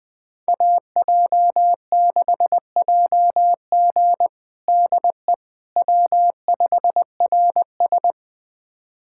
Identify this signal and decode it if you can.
Morse code, heard as a keyed tone, about 20 wpm, 690 Hz. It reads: AJ6JG DE W5RS
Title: USA Callsign DE Callsign 20wpm